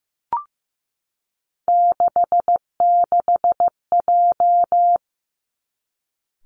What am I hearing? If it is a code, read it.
Morse code, heard as a keyed tone, about 15 wpm, 700 Hz. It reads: 66J
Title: Three Challenging Letter-Number Combinations 15wpm